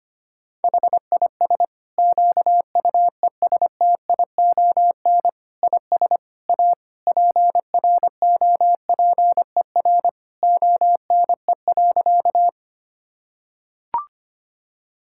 Morse code, heard as a keyed tone, about 25 wpm, 700 Hz. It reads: HIS QUESTION IS A PROPER ONE.
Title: Sentences from Top 1000 Words - Review Part 5 - 25wpm